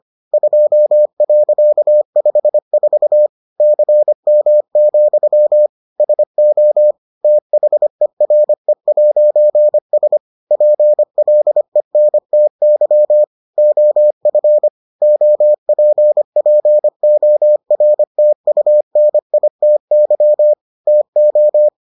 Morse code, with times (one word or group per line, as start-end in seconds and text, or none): 0.3-3.3 2.54
3.6-5.7 CM,
6.0-6.9 SO
7.2-10.2 THERE'S
10.5-13.2 PLENTY
13.6-14.7 OF
15.0-20.5 OPPORTUNITY
20.9-21.7 TO